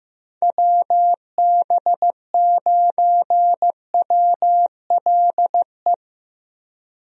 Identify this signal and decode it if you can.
Morse code, heard as a keyed tone, about 16 wpm, 700 Hz. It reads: WB9WLE
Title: USA Call Signs - 6 Characters - Repeated 3x - 15wpm